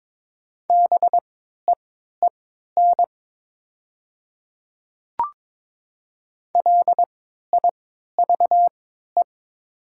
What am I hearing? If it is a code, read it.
Morse code, heard as a keyed tone, about 22 wpm, spaced as for 13 wpm, 700 Hz. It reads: BEEN LIVE
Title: 3x Character Spacing - Top 500 Words - 22wpm